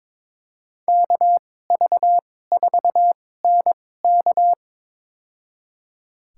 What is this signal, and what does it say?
Morse code, heard as a keyed tone, about 22 wpm, spaced as for 16 wpm, 700 Hz. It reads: KV4NK
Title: USA Call Signs - 5 Characters - 2x Character Spacing - 22wpm